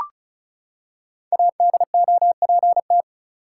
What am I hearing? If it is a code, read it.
Morse code, heard as a keyed tone, about 36 wpm, 700 Hz. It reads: ADOPT
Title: Five Letter Words - 35wpm